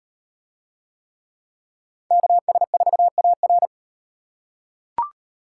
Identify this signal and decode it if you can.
Morse code, heard as a keyed tone, about 37 wpm, 700 Hz. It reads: KS4AR